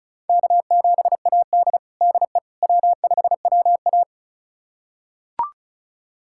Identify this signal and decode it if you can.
Morse code, heard as a keyed tone, about 34 wpm, 700 Hz. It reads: K7AD DE W5WA
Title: USA Callsign DE Callsign - 4 Characters - 35wpm